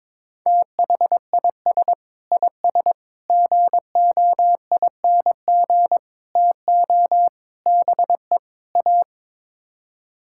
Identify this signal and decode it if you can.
Morse code, heard as a keyed tone, about 22 wpm, 700 Hz. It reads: THIS IS GOING TO BE A
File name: practice / 22wpm / Sets of 6 Words - 22wpm